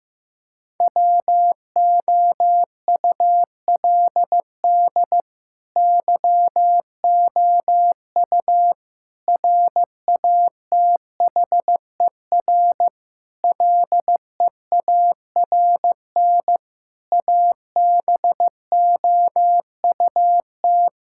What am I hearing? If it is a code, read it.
Morse code, heard as a keyed tone, about 15 wpm, 700 Hz. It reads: WOULD YOU RATHER LEARN ABOUT
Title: Sentences from Top 1000 Words - Review Part 6 - 15wpm